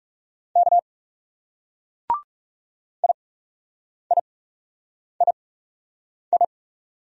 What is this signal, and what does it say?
Morse code, heard as a keyed tone, about 38 wpm, spaced as for 8 wpm, 700 Hz. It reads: K IIII